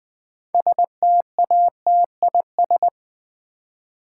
Morse code, heard as a keyed tone, about 20 wpm, 700 Hz. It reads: STATIS
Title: Build A Word - 3 Letters At A Time - 9-Letter Words - 20wpm